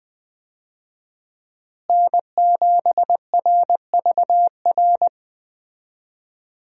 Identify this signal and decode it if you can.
Morse code, heard as a keyed tone, about 20 wpm, 700 Hz. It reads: N7RVR
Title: USA Call Signs - 5 Characters - Repeated 2x - 20wpm